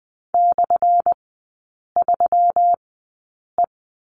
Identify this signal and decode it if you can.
Morse code, heard as a keyed tone, about 20 wpm, spaced as for 8 wpm, 700 Hz. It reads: /3E
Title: 178 Lesson 29e - Slash - Three Characters 20wpm